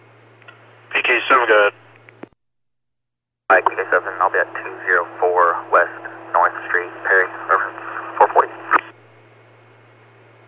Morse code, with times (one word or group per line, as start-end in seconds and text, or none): none